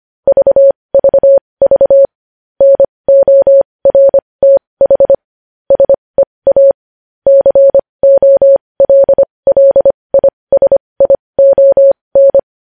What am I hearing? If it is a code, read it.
Morse code, heard as a keyed tone, about 25 wpm, spaced as for 20 wpm, 550 Hz. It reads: VVV NORTH SEA COLLISION
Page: News Headlines in Morse Code at 20 WPM